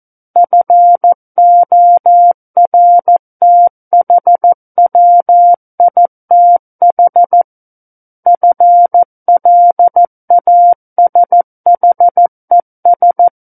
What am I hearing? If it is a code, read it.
Morse code, heard as a keyed tone, about 14 wpm, 700 Hz. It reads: FORTHWITH FLASHES
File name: Book One / 05-Chapter 5 - 14 WPM